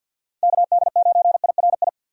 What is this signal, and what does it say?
Morse code, heard as a keyed tone, about 50 wpm, 700 Hz. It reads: KD9IRI
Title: USA Call Signs - 6 Characters - 50wpm